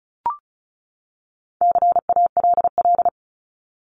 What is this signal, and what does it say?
Morse code, heard as a keyed tone, about 34 wpm, 700 Hz. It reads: CALL
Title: Top 300 Words 35wpm